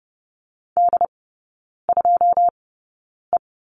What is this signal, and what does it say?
Morse code, heard as a keyed tone, about 30 wpm, spaced as for 9 wpm, 700 Hz. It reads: D2E
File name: 081 Lesson 14e - Number 2 - Three Characters 30wpm